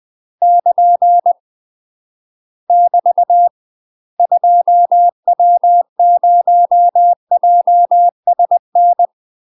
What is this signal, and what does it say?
Morse code, heard as a keyed tone, about 20 wpm, 700 Hz. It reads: <KN> <BT> 2W0JSN
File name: UK MASTER QSOs TRAINING / 20WPM / UK MASTER QSOs 0017